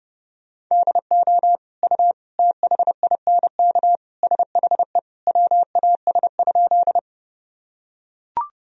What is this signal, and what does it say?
Morse code, heard as a keyed tone, about 32 wpm, 700 Hz. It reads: DO U THINK SHE WAS?
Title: Sentences from Top 200 Words and Abbreviations 30wpm